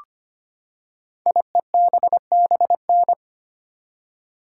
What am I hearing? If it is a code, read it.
Morse code, heard as a keyed tone, about 25 wpm, 700 Hz. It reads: IEBBN
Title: Code Groups - 5 Letters - 25wpm